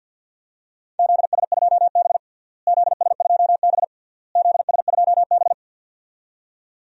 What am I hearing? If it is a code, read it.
Morse code, heard as a keyed tone, about 50 wpm, 700 Hz. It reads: ZS2B ZS2B ZS2B
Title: DXCC-100 Call Signs - 4 Characters - Repeated 3x - 50wpm